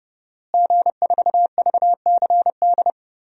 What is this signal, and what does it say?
Morse code, heard as a keyed tone, about 30 wpm, 700 Hz. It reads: G4VCD
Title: DXCC-100 Call Signs - 5 Characters - Repeated 2x - 30wpm